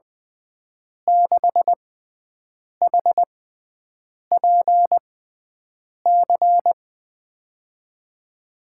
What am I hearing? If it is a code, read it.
Morse code, heard as a keyed tone, about 20 wpm, spaced as for 7 wpm, 700 Hz. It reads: 6HPC
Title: USA Call Signs - 5 Characters - 6x Character Spacing - 20wpm